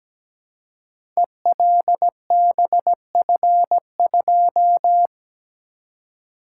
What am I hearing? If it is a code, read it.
Morse code, heard as a keyed tone, about 17 wpm, 700 Hz. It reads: ELBF2
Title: Code Groups - 5 Letter:Numbers - 17wpm